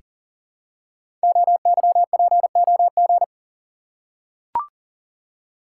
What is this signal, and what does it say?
Morse code, heard as a keyed tone, about 42 wpm, 700 Hz. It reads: OYPOG
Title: Code Groups - 5 Letters - 40wpm